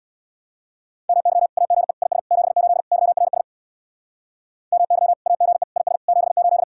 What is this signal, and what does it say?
Morse code, heard as a keyed tone, about 64 wpm, 700 Hz. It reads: MY AGE IS 78 YRS MY AGE IS 78